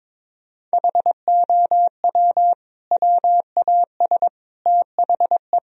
Morse code, heard as a keyed tone, about 22 wpm, 700 Hz. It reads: HOW WAS THE
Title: Sentences from Top 1000 Words - Review Part 4 - 22wpm